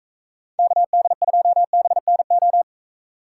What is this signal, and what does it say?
Morse code, heard as a keyed tone, about 33 wpm, 700 Hz. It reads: KD2BNO
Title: Speed Racing - USA 2x3 Call Signs 28wpm